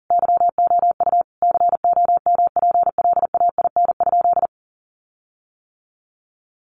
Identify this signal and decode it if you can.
Morse code, heard as a keyed tone, about 40 wpm, 700 Hz. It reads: YOU COMPLAIN?
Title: Sentences from Top 1300 Words 40wpm